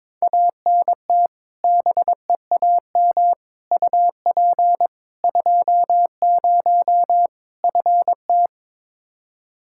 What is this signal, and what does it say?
Morse code, heard as a keyed tone, about 22 wpm, 700 Hz. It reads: ANT BEAM UP 20 FT
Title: K1USN SST QSO Practice 22wpm